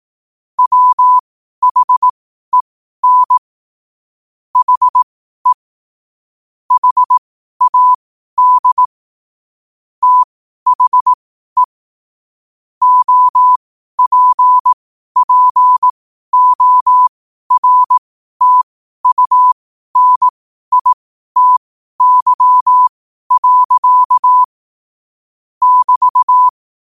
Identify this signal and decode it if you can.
Morse code, heard as a keyed tone, about 18 wpm, spaced as for 12 wpm, 1000 Hz. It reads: WHEN HE HAD THE OPPORTUNITY. <BT>
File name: qotdfiles / 12wpm